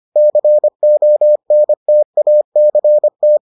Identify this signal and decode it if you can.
Morse code, heard as a keyed tone, about 25 wpm, 600 Hz. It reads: CONTACT